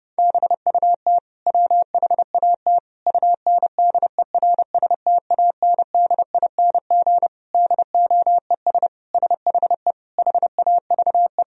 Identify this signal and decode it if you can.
Morse code, heard as a keyed tone, about 29 wpm, 700 Hz. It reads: BUT WHAT UNDERSTANDING DOES SHE HAVE
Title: Sentences from Top 1000 Words - Review Part 3 - 30wpm